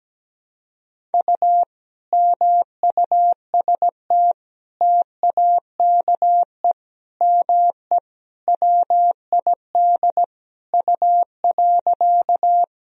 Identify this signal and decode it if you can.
Morse code, heard as a keyed tone, about 17 wpm, 700 Hz. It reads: U MUST TAKE ME WID U.
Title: Sentences from Top 200 Words and Abbreviations 17wpm